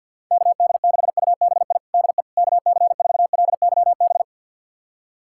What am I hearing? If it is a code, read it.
Morse code, heard as a keyed tone, about 52 wpm, 700 Hz. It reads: KD6UBI DE KC4LYB
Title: USA Callsign DE Callsign - 6 Characters - 50wpm